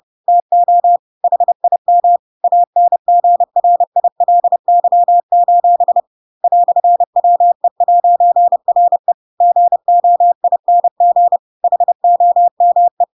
Morse code, heard as a keyed tone, about 30 wpm, 700 Hz. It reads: TO HIM ANGRILY: "WE'RE GOING HOME
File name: Pinocchio / Pinocchio ch3 30 30 ik1hge